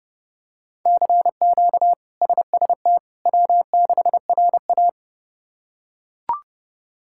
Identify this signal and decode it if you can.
Morse code, heard as a keyed tone, about 29 wpm, 700 Hz. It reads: CQ SST W6RA
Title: K1USN SST QSO Practice - 30wpm